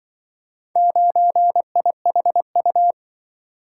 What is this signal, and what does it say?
Morse code, heard as a keyed tone, about 25 wpm, 700 Hz. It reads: 9IHU